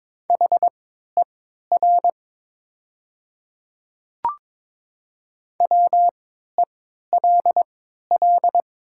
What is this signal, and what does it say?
Morse code, heard as a keyed tone, about 22 wpm, spaced as for 12 wpm, 700 Hz. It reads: HER WELL
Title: 3x Character Spacing - Top 200 Words - 22wpm